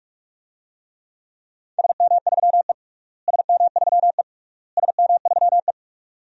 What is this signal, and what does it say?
Morse code, heard as a keyed tone, about 46 wpm, 700 Hz. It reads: SM3E SM3E SM3E
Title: DXCC-100 Call Signs - 4 Characters - Repeated 3x - 45wpm